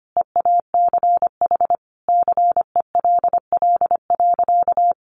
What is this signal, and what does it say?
Morse code, heard as a keyed tone, about 25 wpm, 700 Hz. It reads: EACH CELL.
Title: Sentences from Top 1100 Words 25wpm